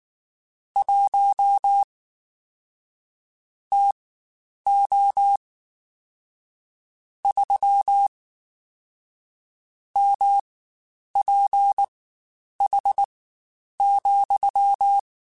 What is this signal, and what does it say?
Morse code, heard as a keyed tone, about 19 wpm, spaced as for 9 wpm, 800 Hz. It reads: 1 TO 3 MPH,
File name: Spaced Characters / Spaced Characters-mp3s / Weather5 5 sp @18